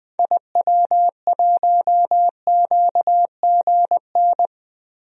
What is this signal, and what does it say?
Morse code, heard as a keyed tone, about 20 wpm, 700 Hz. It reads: IW1QGN
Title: DXCC-100 Call Signs - 6 Characters - Repeated 2x - 20wpm